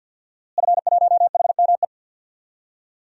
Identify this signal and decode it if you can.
Morse code, heard as a keyed tone, about 53 wpm, 700 Hz. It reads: F1HME
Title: DXCC-100 Call Signs - 5 Characters - Repeated 3x - 50wpm